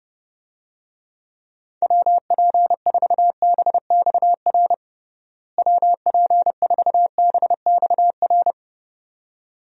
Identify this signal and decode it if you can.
Morse code, heard as a keyed tone, about 30 wpm, 700 Hz. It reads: WP4BXR WP4BXR
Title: USA Call Signs - 6 Characters - Repeated 2x - 30wpm